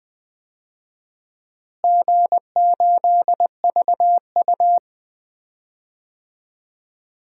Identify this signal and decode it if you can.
Morse code, heard as a keyed tone, about 20 wpm, 700 Hz. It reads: G8VU
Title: DXCC-100 Call Signs - 4 Characters - Repeated 3x - 20wpm